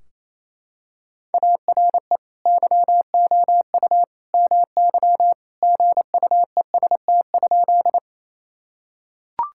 Morse code, heard as a keyed tone, about 28 wpm, 700 Hz. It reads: ARE YOU MY GUEST?
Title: Sentences from Top 1600 Words 28wpm